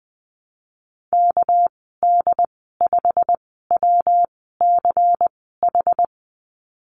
Morse code, heard as a keyed tone, about 20 wpm, spaced as for 15 wpm, 700 Hz. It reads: KD5WCH
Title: 213 Lesson 34g - Number 8 - Call Signs 20wpm